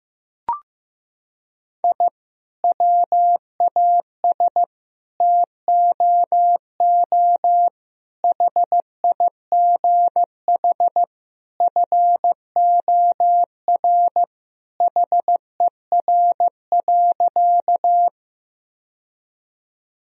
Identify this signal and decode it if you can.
Morse code, heard as a keyed tone, about 15 wpm, 700 Hz. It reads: I WAS TOO HIGH FOR HER.
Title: Sentences from Top 1000 Words - Review Part 1 - 15wpm